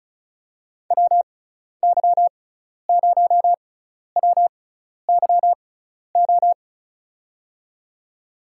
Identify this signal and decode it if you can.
Morse code, heard as a keyed tone, about 36 wpm, spaced as for 12 wpm, 700 Hz. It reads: WY0WYO
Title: USA Call Signs - 6 Characters - 6x Character Spacing - 35wpm